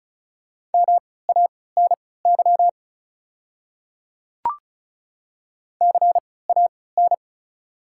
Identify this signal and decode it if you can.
Morse code, heard as a keyed tone, about 36 wpm, spaced as for 20 wpm, 690 Hz. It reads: MANY CAN
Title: 3x Character Spacing - Top 100 Words - 35wpm